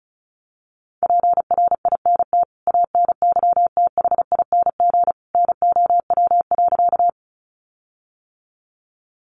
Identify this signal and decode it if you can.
Morse code, heard as a keyed tone, about 34 wpm, 700 Hz. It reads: PRINT ANYTHING NOW.
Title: Sentences from Top 1400 Words 35wpm